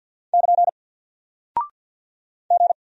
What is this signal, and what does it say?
Morse code, heard as a keyed tone, about 50 wpm, 700 Hz. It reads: <KN> G
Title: Single Character (Letter:Number:Punctuation:Prosign) Alt 50wpm